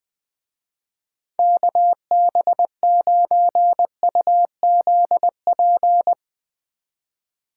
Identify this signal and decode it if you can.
Morse code, heard as a keyed tone, about 19 wpm, 700 Hz. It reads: KB9UZP
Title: Speed Racing - USA 2x3 Call Signs 20wpm